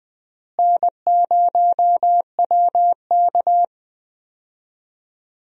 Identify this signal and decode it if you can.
Morse code, heard as a keyed tone, about 20 wpm, 700 Hz. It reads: N0WK
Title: USA Call Signs - 4 Characters - Repeated 2x - 20wpm